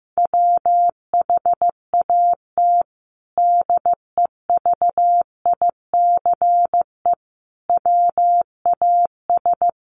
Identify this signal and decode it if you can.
Morse code, heard as a keyed tone, about 15 wpm, 700 Hz. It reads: WHAT DEVICE WAS
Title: Sentences from Top 1500 Words 15wpm